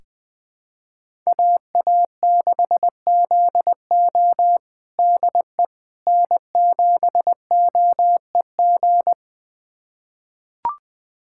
Morse code, heard as a keyed tone, about 20 wpm, 700 Hz. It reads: AA6ZO DE N7OEG
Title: USA Callsign DE Callsign - 5 Characters - 20wpm